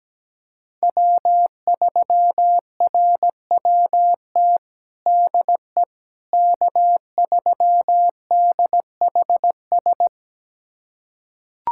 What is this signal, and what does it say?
Morse code, heard as a keyed tone, about 17 wpm, 700 Hz. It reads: W3RWT DE K3DHS
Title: USA Callsign DE Callsign - 5 Characters - 17wpm